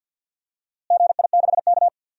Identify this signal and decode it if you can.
Morse code, heard as a keyed tone, about 45 wpm, 700 Hz. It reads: GI6K
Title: DXCC-100 Call Signs - 4 Characters - Repeated 3x - 50wpm